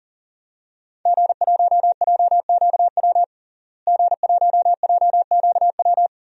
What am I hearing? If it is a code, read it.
Morse code, heard as a keyed tone, about 40 wpm, 700 Hz. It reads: G1JQW G1JQW
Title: DXCC-100 Call Signs - 5 Characters - Repeated 3x - 40wpm